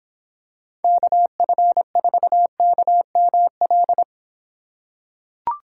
Morse code, heard as a keyed tone, about 26 wpm, 700 Hz. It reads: KF4KML